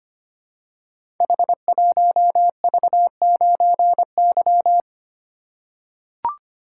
Morse code, heard as a keyed tone, about 25 wpm, 700 Hz. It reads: H1V9Y
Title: Code Groups - 5 Letter:Numbers - 25wpm